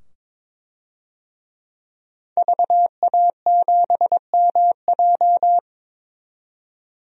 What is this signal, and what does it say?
Morse code, heard as a keyed tone, about 21 wpm, 700 Hz. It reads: VA7MJ
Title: DXCC-100 Call Signs - 5 Characters - 22wpm